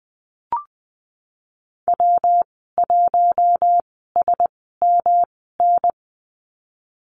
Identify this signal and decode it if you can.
Morse code, heard as a keyed tone, about 20 wpm, spaced as for 15 wpm, 700 Hz. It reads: W1SMN
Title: 128 Lesson 21g - Number 6 - Call Signs 20wpm